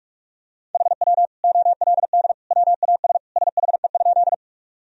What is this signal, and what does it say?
Morse code, heard as a keyed tone, about 41 wpm, 700 Hz. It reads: HW OLD WAS SHE?
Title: Sentences from Top 200 Words and Abbreviations 45wpm